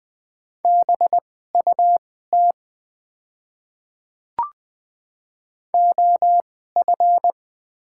Morse code, heard as a keyed tone, about 20 wpm, spaced as for 14 wpm, 700 Hz. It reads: BUT OF
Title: 2x Character Spacing - Top 200 Words - 20wpm